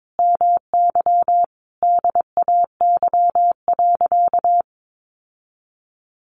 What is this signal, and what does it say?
Morse code, heard as a keyed tone, about 22 wpm, 700 Hz. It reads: MY DAY.
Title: Sentences from Top 1100 Words 22wpm